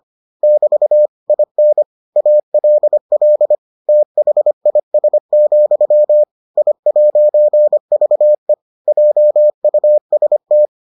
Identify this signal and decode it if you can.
Morse code, heard as a keyed tone, about 25 wpm, 600 Hz. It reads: <BT> IN ALL THIS, I'VE JUST